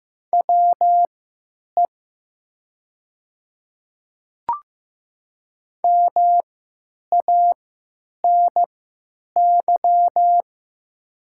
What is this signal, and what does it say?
Morse code, heard as a keyed tone, about 15 wpm, spaced as for 9 wpm, 700 Hz. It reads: WE MANY
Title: 3x Character Spacing - Top 100 Words - 15wpm